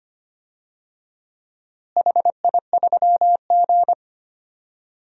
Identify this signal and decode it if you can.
Morse code, heard as a keyed tone, about 25 wpm, 700 Hz. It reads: HI3G
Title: DXCC-100 Call Signs - 4 Characters - Repeated 2x - 25wpm